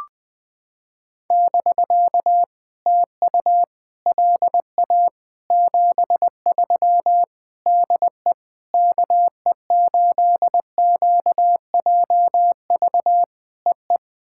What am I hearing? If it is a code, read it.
Morse code, heard as a keyed tone, about 21 wpm, 700 Hz. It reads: <BK> TU LA 73 DE KE8QJV EE
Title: POTA QSO Practice - 20wpm